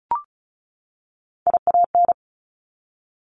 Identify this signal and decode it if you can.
Morse code, heard as a keyed tone, about 36 wpm, 700 Hz. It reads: IAN
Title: Top Three Letter Combinations in Words 35wpm